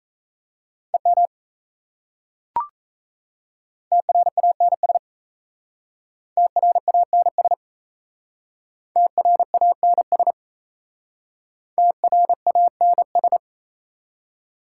Morse code, heard as a keyed tone, about 35 wpm, 700 Hz. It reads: EM TRANS TRANS TRANS TRANS
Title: Speed-Racing Prefixes and Suffixes 28wpm